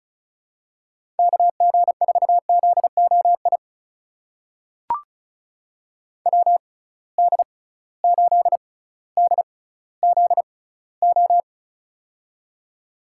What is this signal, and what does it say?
Morse code, heard as a keyed tone, about 36 wpm, spaced as for 18 wpm, 700 Hz. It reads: KG4ZOI WD8DZO
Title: USA Call Signs - 6 Characters - 6x Character Spacing - 35wpm